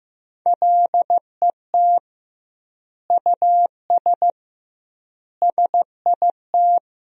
Morse code, heard as a keyed tone, about 14 wpm, 700 Hz. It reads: LET US SIT
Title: 2x Word-Spacing - Sentences from Top 500 Words - 15wpm